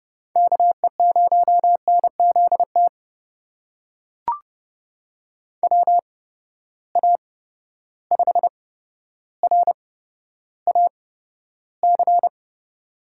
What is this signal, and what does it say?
Morse code, heard as a keyed tone, about 32 wpm, spaced as for 13 wpm, 700 Hz. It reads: KE0NZT WA5RAC
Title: USA Call Signs - 6 Characters - 8x Character Spacing - 30wpm